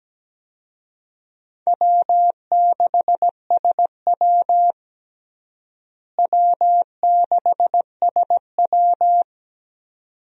Morse code, heard as a keyed tone, about 17 wpm, 700 Hz. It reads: W6SW W6SW
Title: USA Call Signs - 4 Characters - Repeated 2x - 17wpm